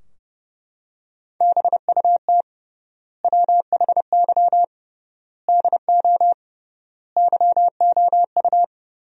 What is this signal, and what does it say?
Morse code, heard as a keyed tone, about 32 wpm, 700 Hz. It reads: BUT WHY DO YOU
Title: 3x Word-Spacing - Sentences from Top 200 Words - 30wpm